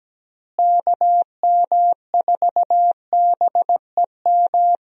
Morse code, heard as a keyed tone, about 17 wpm, 700 Hz. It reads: KM4BEM
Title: USA Call Signs - 6 Characters - Repeated 2x - 17wpm